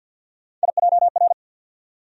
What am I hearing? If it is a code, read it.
Morse code, heard as a keyed tone, about 50 wpm, 700 Hz. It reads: IJR
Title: USA Call Sign Postfixes - 3 Letters 50wpm